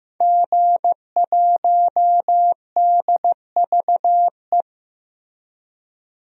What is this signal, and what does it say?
Morse code, heard as a keyed tone, about 15 wpm, 700 Hz. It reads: G1DVE
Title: DXCC-100 Call Signs - 5 Characters - 15wpm